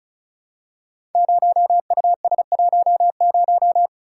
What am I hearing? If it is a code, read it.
Morse code, heard as a keyed tone, about 36 wpm, 690 Hz. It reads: 0US10
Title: Code Groups - 5 Letter:Numbers - 35wpm